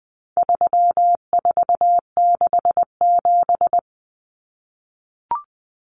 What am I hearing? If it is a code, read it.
Morse code, heard as a keyed tone, about 20 wpm, 700 Hz. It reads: 3467
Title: Numbers - 4-Digit 20wpm